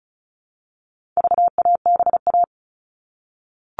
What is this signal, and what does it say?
Morse code, heard as a keyed tone, about 34 wpm, 700 Hz. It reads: VABA
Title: All Cut Numbers - 4-Digit 35wpm